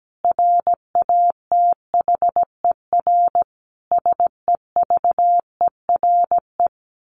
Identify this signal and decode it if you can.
Morse code, heard as a keyed tone, about 17 wpm, 700 Hz. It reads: RATHER SEVERE
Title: Sentences from Top 1400 Words 17wpm